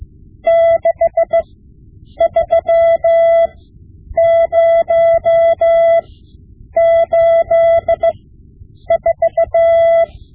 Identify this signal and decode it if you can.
Morse code, heard as a keyed tone, about 16 wpm, spaced as for 8 wpm, 670 Hz. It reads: 63084
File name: Morse Lessons / Morse Lessons / Tape-5B